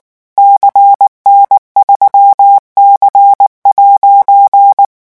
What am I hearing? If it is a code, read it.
Morse code, heard as a keyed tone, about 19 wpm, 800 Hz. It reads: CN3C'
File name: morse mp3 / revision15-vitesse 18 mots